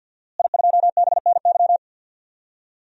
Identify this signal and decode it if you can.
Morse code, heard as a keyed tone, about 52 wpm, 700 Hz. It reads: I2BNY
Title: DXCC-100 Call Signs - 5 Characters - Repeated 3x - 50wpm